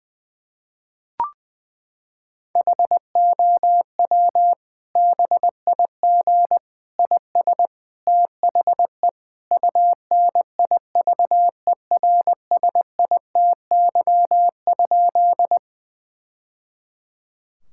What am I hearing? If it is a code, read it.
Morse code, heard as a keyed tone, about 21 wpm, 700 Hz. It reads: HOW BIG IS THE UNIVERSITY?
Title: Sentences from Top 1000 Words - Review Part 5 - 20wpm